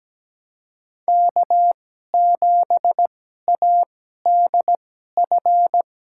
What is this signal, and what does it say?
Morse code, heard as a keyed tone, about 17 wpm, spaced as for 12 wpm, 700 Hz. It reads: K7ADF
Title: USA Call Signs - 5 Characters - 2x Character Spacing - 17wpm